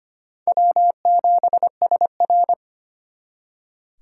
Morse code, heard as a keyed tone, about 34 wpm, 700 Hz. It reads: W7SR